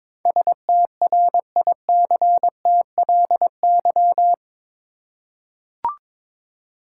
Morse code, heard as a keyed tone, about 22 wpm, 700 Hz. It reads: STRICTLY
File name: practice / 22wpm / Eight Letter Words - 22wpm